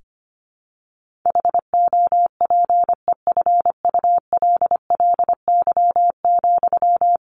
Morse code, heard as a keyed tone, about 25 wpm, 700 Hz. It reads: HOPEFULLY,
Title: Sentences from Top 1500 Words 25wpm